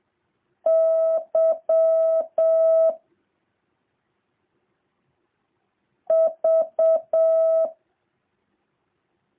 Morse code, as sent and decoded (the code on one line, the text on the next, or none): -.-- / ...-
Y V